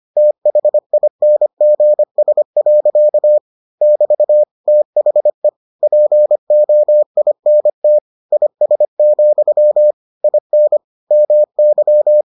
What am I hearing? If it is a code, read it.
Morse code, heard as a keyed tone, about 25 wpm, 600 Hz. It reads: THINGS. <BT> THE POINT IS, IN MY